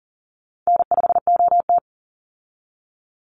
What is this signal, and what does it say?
Morse code, heard as a keyed tone, about 40 wpm, 700 Hz. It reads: N5OT
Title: Easy Call Signs Repeated 3x - 40wpm